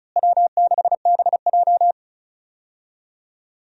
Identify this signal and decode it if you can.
Morse code, heard as a keyed tone, about 34 wpm, 700 Hz. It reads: W6BJ
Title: USA Call Signs - 4 Characters - Repeated 2x - 35wpm